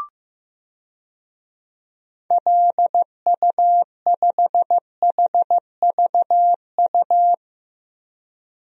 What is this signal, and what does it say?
Morse code, heard as a keyed tone, about 15 wpm, 700 Hz. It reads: LU5HVU
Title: DXCC-100 Call Signs - 6 Characters - Repeated 2x - 15wpm